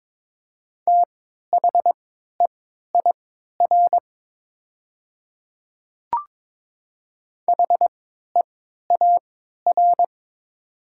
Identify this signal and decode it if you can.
Morse code, heard as a keyed tone, about 22 wpm, spaced as for 12 wpm, 700 Hz. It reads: THEIR HEAR
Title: 3x Character Spacing - Top 300 Words - 22wpm